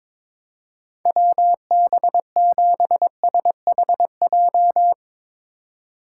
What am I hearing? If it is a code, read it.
Morse code, heard as a keyed tone, about 22 wpm, 700 Hz. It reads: WB7SHJ
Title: USA Call Signs - 6 Characters - 3x Character Spacing - 22wpm